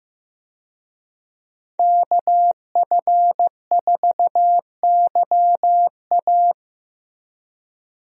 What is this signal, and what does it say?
Morse code, heard as a keyed tone, about 16 wpm, 700 Hz. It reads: KF4YA
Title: USA Call Signs - 5 Characters - Repeated 3x - 15wpm